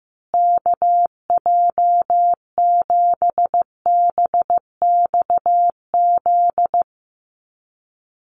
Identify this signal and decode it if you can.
Morse code, heard as a keyed tone, about 15 wpm, 700 Hz. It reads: KJ7BXZ
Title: USA Call Signs Repeated 3x 15wpm